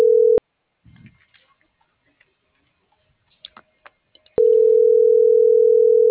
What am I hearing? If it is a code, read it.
Ringback tone (North America), 440 + 480 Hz.